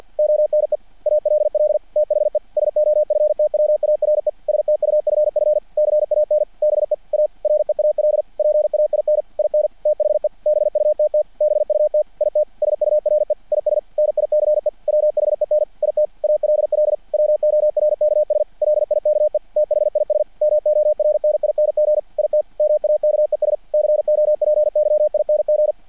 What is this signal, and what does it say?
Morse code, heard as a keyed tone, about 50 wpm, 600 Hz. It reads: ONE ALL THE SOFTWARE STUFF CAN BE A REAL PAIN IN THE BUTT BUT IT SURE IS NICE WHEN IT ALL WORKS LIKE THIS MORNING IT MAKES COPYING
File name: archived / CRSnet